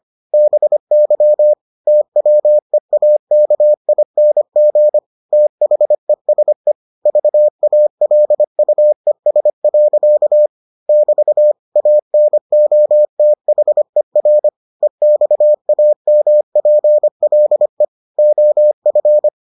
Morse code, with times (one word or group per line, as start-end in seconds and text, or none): 0.3-1.5 BY
1.9-5.0 TWEAKING
5.3-6.7 THESE
7.1-10.5 VALUES.
10.9-11.5 <BT>
11.8-14.5 ANOTHER
14.8-17.9 EXAMPLE
18.2-19.3 OF